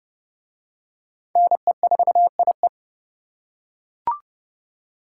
Morse code, heard as a keyed tone, about 27 wpm, 700 Hz. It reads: NE4IE